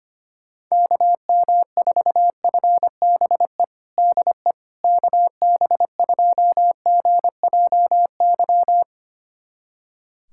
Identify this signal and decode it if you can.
Morse code, heard as a keyed tone, about 25 wpm, 700 Hz. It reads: KM4FBE DE KB2GJY
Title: USA Callsign DE Callsign - 6 Characters - 25wpm